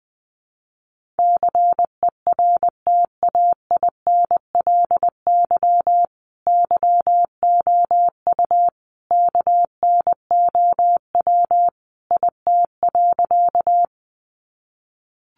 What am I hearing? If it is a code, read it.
Morse code, heard as a keyed tone, about 20 wpm, 700 Hz. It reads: CERTAINLY YOU KNOW IT.
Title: Sentences from Top 900 Words 20wpm